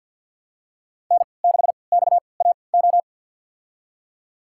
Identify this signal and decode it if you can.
Morse code, heard as a keyed tone, about 52 wpm, spaced as for 29 wpm, 700 Hz. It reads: N6XAO
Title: USA Call Signs - 3x Character Spacing - 50wpm